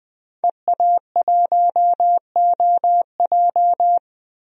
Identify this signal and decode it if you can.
Morse code, heard as a keyed tone, about 20 wpm, 700 Hz. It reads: EA1OJ
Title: DXCC-100 Call Signs - 5 Characters - 20wpm